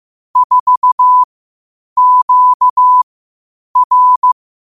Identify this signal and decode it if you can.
Morse code, heard as a keyed tone, about 15 wpm, spaced as for 8 wpm, 1000 Hz. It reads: 4QR
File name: qotdfiles / 8wpm